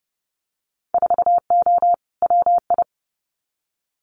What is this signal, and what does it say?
Morse code, heard as a keyed tone, about 30 wpm, 700 Hz. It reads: 4O WI
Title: Winter Field Day Exchanges 30wpm